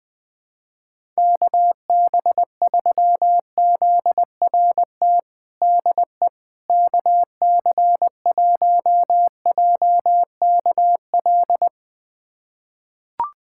Morse code, heard as a keyed tone, about 20 wpm, 700 Hz. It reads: KB3ZRT DE KC1JKL
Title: USA Callsign DE Callsign 20wpm